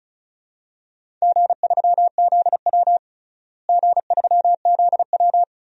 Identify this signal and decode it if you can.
Morse code, heard as a keyed tone, about 36 wpm, 700 Hz. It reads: G3ZW G3ZW
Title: DXCC-100 Call Signs - 4 Characters - Repeated 3x - 35wpm